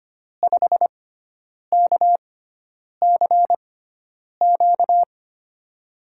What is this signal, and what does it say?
Morse code, heard as a keyed tone, about 25 wpm, spaced as for 9 wpm, 700 Hz. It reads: 5KCQ
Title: USA Call Signs - 5 Characters - 6x Character Spacing - 25wpm